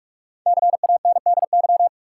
Morse code, heard as a keyed tone, about 46 wpm, 700 Hz. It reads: CANDY
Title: Five Letter Words - 45wpm